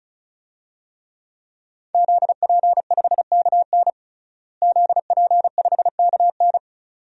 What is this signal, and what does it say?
Morse code, heard as a keyed tone, about 36 wpm, 700 Hz. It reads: ZP5KN ZP5KN
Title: DXCC-100 Call Signs - 5 Characters - Repeated 2x - 35wpm